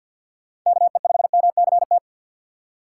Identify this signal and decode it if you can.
Morse code, heard as a keyed tone, about 52 wpm, 700 Hz. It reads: KE5MCT
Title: USA Call Signs - 6x Character Spacing - 50wpm